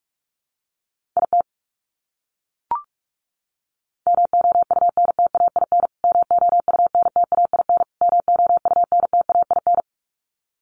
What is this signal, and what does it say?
Morse code, heard as a keyed tone, about 44 wpm, 700 Hz. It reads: IT MOUNTAIN MOUNTAIN MOUNTAIN
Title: Top 500 Words Repeated 45wpm